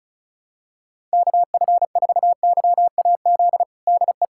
Morse code, heard as a keyed tone, about 36 wpm, 700 Hz. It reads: KF4YAZ DE
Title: USA Callsign DE Callsign 35wpm